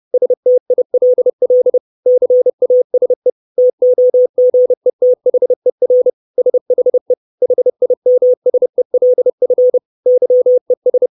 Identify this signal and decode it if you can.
Morse code, heard as a keyed tone, about 30 wpm, 500 Hz. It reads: STILL CASE TOGETHER SHE HIMSELF YES
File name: 632 Words 30WPM